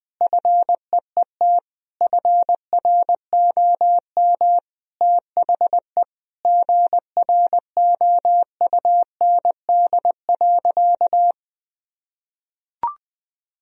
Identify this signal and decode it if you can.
Morse code, heard as a keyed tone, about 20 wpm, 700 Hz. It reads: FEET FROM THE GROUND.
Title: Sentences from Top 1000 Words - Review Part 2 - 20wpm